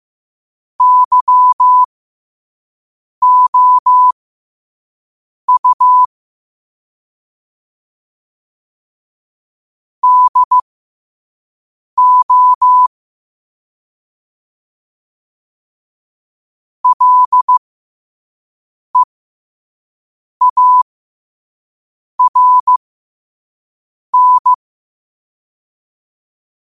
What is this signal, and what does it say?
Morse code, heard as a keyed tone, about 15 wpm, spaced as for 5 wpm, 1000 Hz. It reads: YOU DO LEARN
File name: qotdfiles / 5wpm